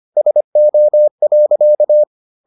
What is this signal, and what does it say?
Morse code, heard as a keyed tone, about 25 wpm, 600 Hz. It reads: SO.